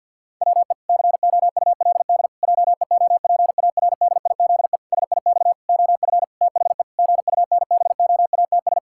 Morse code, heard as a keyed tone, about 52 wpm, 700 Hz. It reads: WE COULD JEOPARDIZE SIX OF THE GUNBOATS
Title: Pangram Challenge v2 - 50wpm